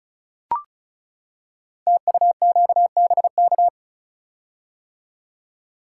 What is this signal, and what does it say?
Morse code, heard as a keyed tone, about 36 wpm, 700 Hz. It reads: TUQBK
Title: Code Groups - 5 Letters - 35wpm